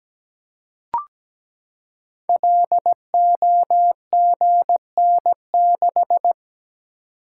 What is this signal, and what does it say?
Morse code, heard as a keyed tone, about 17 wpm, 700 Hz. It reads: LOGN6
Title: Code Groups - 5 Letter:Numbers - 17wpm